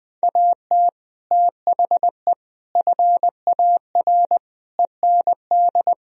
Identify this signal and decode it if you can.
Morse code, heard as a keyed tone, about 21 wpm, 700 Hz. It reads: AT THE FAR END
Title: Speed-Racing - Sets of 5 Words 20wpm